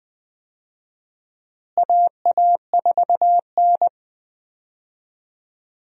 Morse code, heard as a keyed tone, about 21 wpm, 700 Hz. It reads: AA4N
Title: USA Call Signs - 4 Characters - Repeated 3x - 20wpm